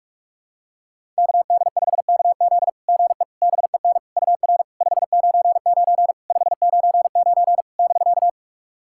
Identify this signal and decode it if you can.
Morse code, heard as a keyed tone, about 46 wpm, 690 Hz. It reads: KD5KZ GE BEN UR 599 599 <BK>
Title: SOTA QSO Practice 45wpm